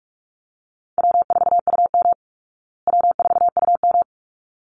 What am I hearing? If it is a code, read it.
Morse code, heard as a keyed tone, about 44 wpm, 700 Hz. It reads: W4UM W4UM
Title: Easy Call Signs Repeated 2x - 45wpm